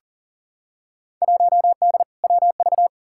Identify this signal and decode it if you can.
Morse code, heard as a keyed tone, about 39 wpm, 700 Hz. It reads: 1D WV
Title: Field Day Exchanges - 40wpm